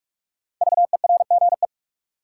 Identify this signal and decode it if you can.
Morse code, heard as a keyed tone, about 42 wpm, 700 Hz. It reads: VERGE
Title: Sets of 4 Words - New Word Drill - 45wpm